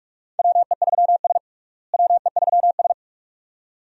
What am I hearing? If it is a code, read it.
Morse code, heard as a keyed tone, about 46 wpm, 700 Hz. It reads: WE3S WE3S
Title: USA Call Signs - 4 Characters - Repeated 2x - 45wpm